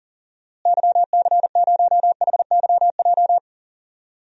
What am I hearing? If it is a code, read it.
Morse code, heard as a keyed tone, about 40 wpm, 700 Hz. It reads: YC0HYJ
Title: DXCC-100 Call Signs - 6 Characters - Repeated 2x - 40wpm